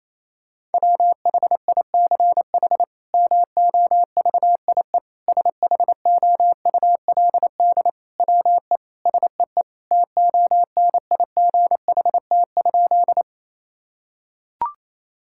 Morse code, heard as a keyed tone, about 28 wpm, 700 Hz. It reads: WHICH MOVIE SHOULD WE SEE TONIGHT?
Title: Sentences from Top 1000 Words - Review Part 5 - 28wpm